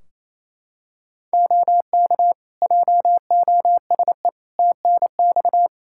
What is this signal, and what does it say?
Morse code, heard as a keyed tone, about 28 wpm, 700 Hz. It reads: OK JOSE TNX
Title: K1USN SST QSO Practice 28wpm